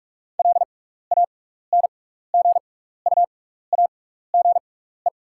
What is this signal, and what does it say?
Morse code, heard as a keyed tone, about 44 wpm, spaced as for 15 wpm, 700 Hz. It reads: LANGUAGE
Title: 6x Character Spacing - Top 500 Words - 45wpm